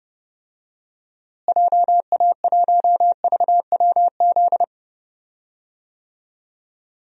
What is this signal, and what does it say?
Morse code, heard as a keyed tone, about 32 wpm, 700 Hz. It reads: JA1VWZ
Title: DXCC-100 Call Signs - 6 Characters - Repeated 3x - 30wpm